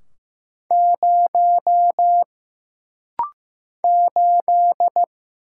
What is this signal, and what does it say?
Morse code, heard as a keyed tone, about 15 wpm, 700 Hz. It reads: 0 8
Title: Single Letter-Number Alt - Rapid-Fire 15wpm